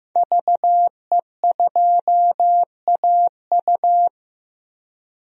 Morse code, heard as a keyed tone, about 15 wpm, 700 Hz. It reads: VE2AU
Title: DXCC-100 Call Signs - 5 Characters - 15wpm